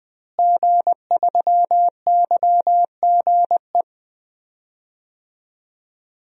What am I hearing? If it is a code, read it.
Morse code, heard as a keyed tone, about 20 wpm, 700 Hz. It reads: G3YGE
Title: DXCC-100 Call Signs - 5 Characters - Repeated 2x - 20wpm